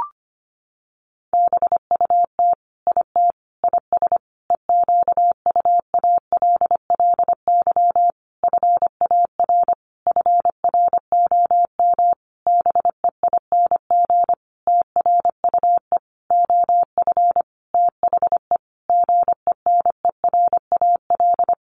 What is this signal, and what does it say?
Morse code, heard as a keyed tone, about 25 wpm, 700 Hz. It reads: BUT IT IS EQUALLY FAR FROM BEING TRUE OF THE GENERAL
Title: Sentences from Top 1400 Words 25wpm